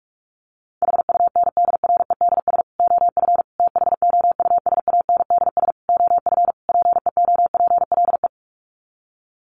Morse code, heard as a keyed tone, about 44 wpm, 700 Hz. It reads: HUNDREDS OF THOUSANDS OF PEOPLE
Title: Sets of 5 Words Encore2 45wpm